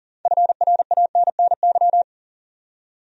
Morse code, heard as a keyed tone, about 40 wpm, 700 Hz. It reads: FRANNY
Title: USA Common Names - 40wpm